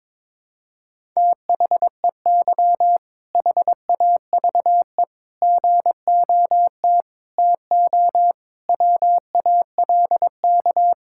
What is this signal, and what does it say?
Morse code, heard as a keyed tone, about 22 wpm, 700 Hz. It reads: THEY HAVE GOT TO WALK
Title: Sentences from Top 1000 Words - Review Part 6 - 22wpm